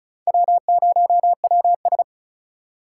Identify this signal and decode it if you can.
Morse code, heard as a keyed tone, about 47 wpm, spaced as for 41 wpm, 700 Hz. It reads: W0WS
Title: Speed Racing - USA 1x2 Call Signs 35wpm